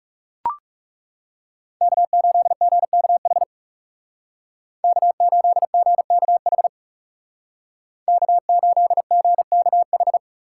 Morse code, heard as a keyed tone, about 39 wpm, 700 Hz. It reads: K8GKH K8GKH K8GKH